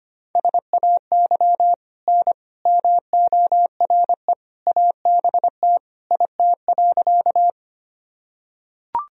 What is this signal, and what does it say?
Morse code, heard as a keyed tone, about 25 wpm, 700 Hz. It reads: SAY N MORE ABT IT.
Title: Sentences from Top 200 Words and Abbreviations 25wpm